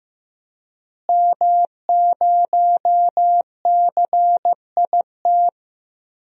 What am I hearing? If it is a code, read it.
Morse code, heard as a keyed tone, about 15 wpm, 700 Hz. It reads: M0CIT
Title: DXCC-100 Call Signs - 5 Characters - Repeated 3x - 15wpm